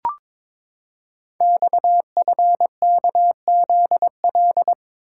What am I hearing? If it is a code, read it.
Morse code, heard as a keyed tone, about 23 wpm, 700 Hz. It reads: XFKZL
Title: Code Groups - 5 Letters - 22wpm